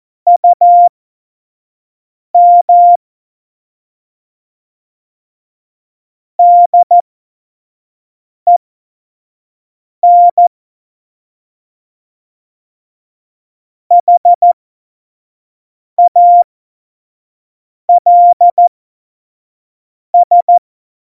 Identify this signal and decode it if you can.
Morse code, heard as a keyed tone, about 14 wpm, spaced as for 5 wpm, 700 Hz. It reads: UM DEN HALS
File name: Morsekurs / Witze / Witz 09. Oktober 2018, 25 F70